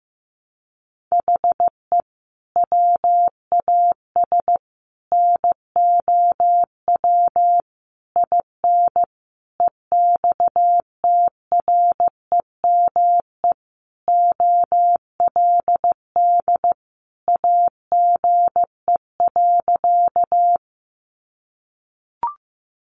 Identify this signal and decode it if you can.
Morse code, heard as a keyed tone, about 15 wpm, 700 Hz. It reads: HE WAS NOW IN EXTREME OLD AGE.
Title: Sentences from Top 1300 Words 15wpm